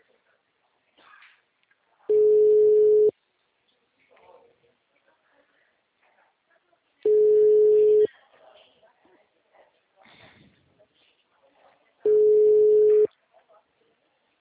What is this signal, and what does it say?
Ringback tone (Europe, 425 Hz), with speech.
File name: out- -2013- - - .60863